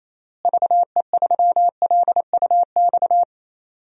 Morse code, heard as a keyed tone, about 28 wpm, 700 Hz. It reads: VE3LUX
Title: DXCC-100 Call Signs - 6 Characters - Repeated 3x - 28wpm